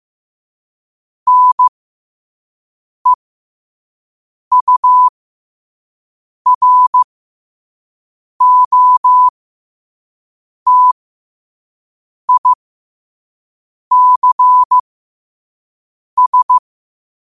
Morse code, heard as a keyed tone, about 15 wpm, spaced as for 5 wpm, 1000 Hz. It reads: NEUROTICS
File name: qotdfiles / 5wpm